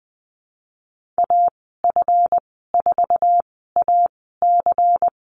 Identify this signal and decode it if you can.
Morse code, heard as a keyed tone, about 20 wpm, spaced as for 15 wpm, 700 Hz. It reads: AF4AC
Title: 161 Lesson 26g - Letter g - Call Signs 20wpm